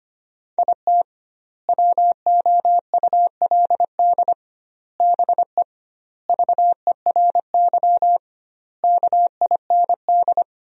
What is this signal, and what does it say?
Morse code, heard as a keyed tone, about 25 wpm, 700 Hz. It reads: IT WOULD BE VERY KIND
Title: 2x Word-Spacing - Sentences from Top 200 Words - 25wpm